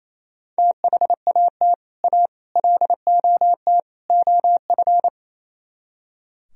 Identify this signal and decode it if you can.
Morse code, heard as a keyed tone, about 35 wpm, 700 Hz. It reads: THAT A LOT OF